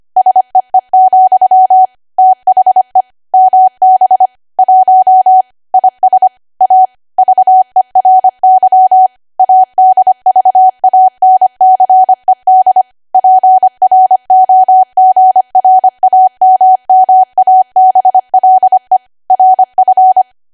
Morse code, heard as a keyed tone, about 25 wpm, 750 Hz. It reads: SEE, THE MB 1 IS A VERY ADVANCED PROGRAMMABLE RF
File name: cwa / W1AW-Level3 / 25WPM